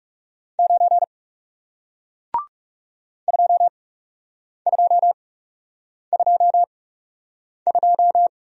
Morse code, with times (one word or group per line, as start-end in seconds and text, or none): 0.6-1.0 9
3.3-8.3 2222